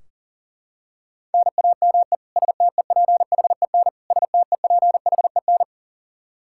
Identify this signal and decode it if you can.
Morse code, heard as a keyed tone, about 40 wpm, 700 Hz. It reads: NAME STEPHEN STEPHEN
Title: Practice QSO - Exchange 1 - Beginning - 40wpm